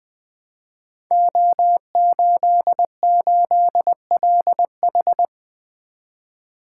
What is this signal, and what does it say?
Morse code, heard as a keyed tone, about 20 wpm, 700 Hz. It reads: O88LH
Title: Code Groups - 5 Letter:Numbers - 20wpm